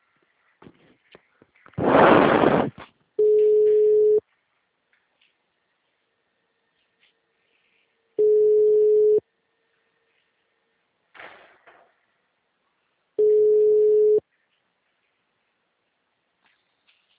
Ringback tone (Europe, 425 Hz).